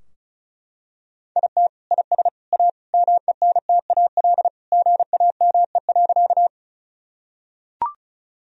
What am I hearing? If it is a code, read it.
Morse code, heard as a keyed tone, about 36 wpm, 700 Hz. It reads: IT IS A MENTAL GAME.
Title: Sentences from Top 1000 Words - Review Part 4 - 35wpm